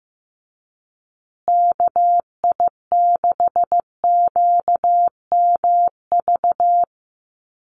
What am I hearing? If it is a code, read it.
Morse code, heard as a keyed tone, about 15 wpm, 700 Hz. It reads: KI6QMV
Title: USA Call Signs Repeated 3x 15wpm